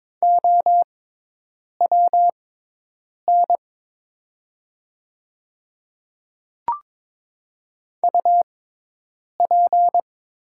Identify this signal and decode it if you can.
Morse code, heard as a keyed tone, about 22 wpm, spaced as for 8 wpm, 690 Hz. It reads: OWN UP
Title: 6x Character Spacing - Top 300 Words - 22wpm